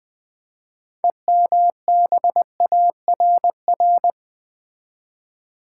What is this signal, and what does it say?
Morse code, heard as a keyed tone, about 20 wpm, 700 Hz. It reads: EMBARR
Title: Build A Word - 6 Letters At A Time - 12-Letter Words - 20wpm